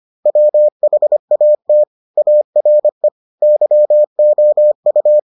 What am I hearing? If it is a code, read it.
Morse code, heard as a keyed tone, about 25 wpm, 600 Hz. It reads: WHAT ARE YOU